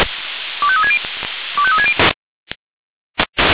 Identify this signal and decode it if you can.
Five-tone selective call (ZVEI), heard as quick runs of five tones, tones of about 70 ms. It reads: 25360 25360